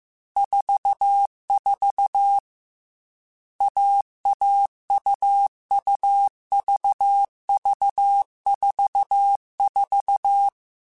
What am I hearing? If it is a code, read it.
Morse code, heard as a keyed tone, about 15 wpm, 800 Hz. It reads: 44 AAUUVV44
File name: Short Runs / Short Runs-mp3s / AUV AUV4 @13